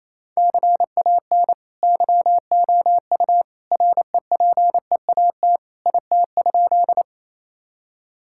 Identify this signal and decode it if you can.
Morse code, heard as a keyed tone, about 28 wpm, 700 Hz. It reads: CAN YOU REPEAT IT?
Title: Sentences from Top 1600 Words 28wpm